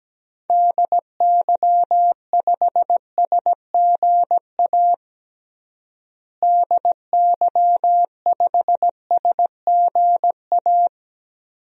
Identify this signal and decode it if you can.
Morse code, heard as a keyed tone, about 17 wpm, 700 Hz. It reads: DY5SGA DY5SGA
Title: DXCC-100 Call Signs - 6 Characters - Repeated 2x - 17wpm